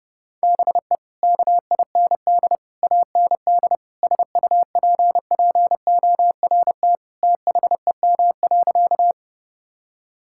Morse code, heard as a keyed tone, about 32 wpm, 700 Hz. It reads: BE KIND AND SUPPORT THEM.
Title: Sentences from Top 1000 Words - Review Part 5 - 30wpm